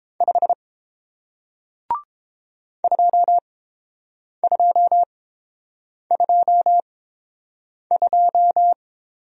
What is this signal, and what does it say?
Morse code, heard as a keyed tone, about 29 wpm, spaced as for 7 wpm, 700 Hz. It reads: 5 2222